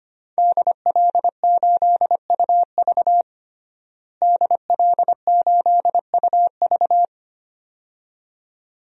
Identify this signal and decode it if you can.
Morse code, heard as a keyed tone, about 25 wpm, 700 Hz. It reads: DL8UV DL8UV
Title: DXCC-100 Call Signs - 5 Characters - Repeated 2x - 25wpm